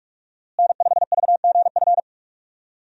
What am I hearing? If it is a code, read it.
Morse code, heard as a keyed tone, about 47 wpm, 700 Hz. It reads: N5VGF
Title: USA Call Signs - 5 Characters - Repeated 3x - 45wpm